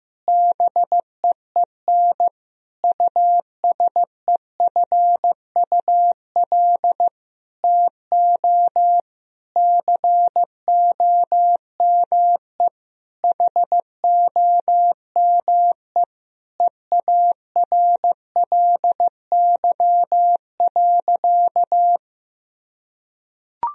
Morse code, heard as a keyed tone, about 15 wpm, 700 Hz. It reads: BEEN USEFUL TO COME HOME EARLY.
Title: Sentences from Top 1000 Words - Review Part 4 - 15wpm